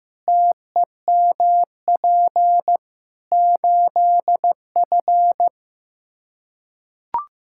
Morse code, heard as a keyed tone, about 15 wpm, 700 Hz. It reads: TEMP 8F
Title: Practice QSO - Exchange 2 - Info - 15wpm